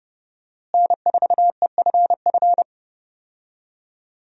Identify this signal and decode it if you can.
Morse code, heard as a keyed tone, about 30 wpm, 700 Hz. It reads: N4EFF
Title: USA Call Signs - 5 Characters - Repeated 3x - 30wpm